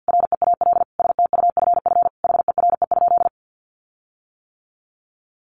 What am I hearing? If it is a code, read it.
Morse code, heard as a keyed tone, about 50 wpm, 700 Hz. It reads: REAL STUFF HERE?
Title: Sentences from Top 1400 Words 50wpm